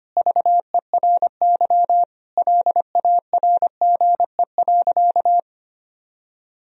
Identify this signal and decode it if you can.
Morse code, heard as a keyed tone, about 25 wpm, 700 Hz. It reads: VERY LARGE.
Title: Sentences from Top 1000 Words - Review Part 1 - 25wpm